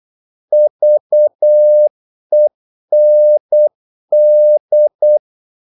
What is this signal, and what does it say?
Morse code, heard as a keyed tone, about 8 wpm, 600 Hz. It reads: VEND
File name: Korte ord DK 8wpm